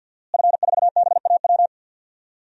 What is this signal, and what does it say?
Morse code, heard as a keyed tone, about 52 wpm, 700 Hz. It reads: F4BAW
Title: DXCC-100 Call Signs - 5 Characters - Repeated 3x - 50wpm